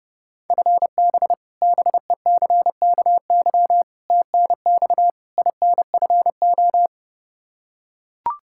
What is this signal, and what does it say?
Morse code, heard as a keyed tone, about 30 wpm, 700 Hz. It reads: FB BECKY TNX INFO
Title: Practice QSO - Exchange 3 - More Info - 30wpm